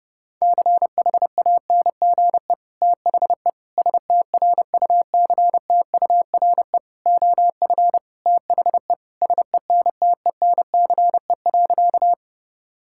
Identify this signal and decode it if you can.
Morse code, heard as a keyed tone, about 30 wpm, 700 Hz. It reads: CHANGE THE STRUCTURE OF THE SENTENCE.
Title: Sentences from Top 1000 Words - Review Part 6 - 30wpm